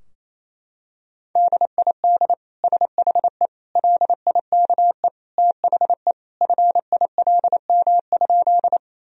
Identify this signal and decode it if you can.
Morse code, heard as a keyed tone, about 28 wpm, 700 Hz. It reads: DID SHE LIKE THE FILM?
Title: Sentences from Top 1000 Words - Review Part 3 - 28wpm